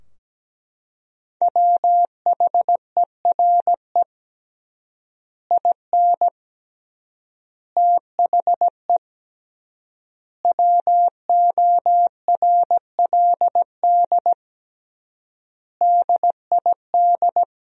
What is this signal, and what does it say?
Morse code, heard as a keyed tone, about 17 wpm, 700 Hz. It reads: WHERE IN THE WORLD DID
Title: 3x Word-Spacing - Sentences from Top 200 Words - 17wpm